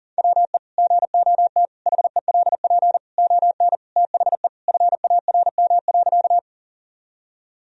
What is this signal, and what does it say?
Morse code, heard as a keyed tone, about 40 wpm, 700 Hz. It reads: WE GOT HELP ON THE FARM.
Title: Sentences from Top 1000 Words - Review Part 2 - 40wpm